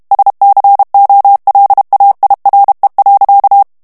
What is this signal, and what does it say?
Morse code, heard as a keyed tone, about 32 wpm, 800 Hz. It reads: SCOLAIRE.
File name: morse mp3 / FRANCAIS TEXTE CLAIR 1 VITESSE 30 MOTS MINUTES